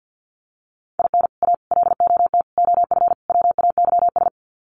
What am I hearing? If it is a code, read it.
Morse code, heard as a keyed tone, about 50 wpm, 700 Hz. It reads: IN A LOT OF WAYS
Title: Sets of 5 Words Encore 50wpm